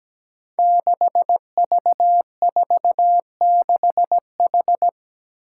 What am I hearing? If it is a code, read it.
Morse code, heard as a keyed tone, about 17 wpm, 700 Hz. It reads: 6V46H
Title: Five Challenging Letter-Number Combinations 17wpm